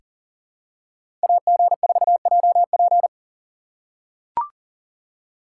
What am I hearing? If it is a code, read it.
Morse code, heard as a keyed tone, about 40 wpm, 700 Hz. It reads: AG4JP
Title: USA Call Signs - 5 Characters - 4x Character Spacing - 40wpm